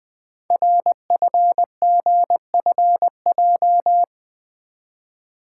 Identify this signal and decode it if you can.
Morse code, heard as a keyed tone, about 20 wpm, 700 Hz. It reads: RFGFJ
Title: Code Groups - 5 Letters - 20wpm